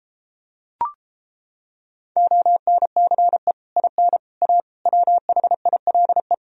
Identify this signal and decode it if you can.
Morse code, heard as a keyed tone, about 29 wpm, 700 Hz. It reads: ONCE IN A WHILE